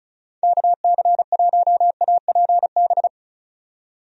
Morse code, heard as a keyed tone, about 36 wpm, 700 Hz. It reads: KC1APB
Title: USA Call Signs - 6 Characters - 35wpm